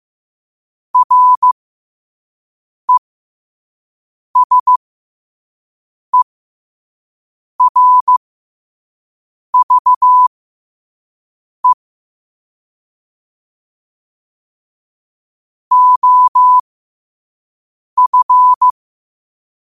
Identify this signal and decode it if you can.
Morse code, heard as a keyed tone, about 15 wpm, spaced as for 5 wpm, 1000 Hz. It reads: RESERVE OF
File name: qotdfiles / 5wpm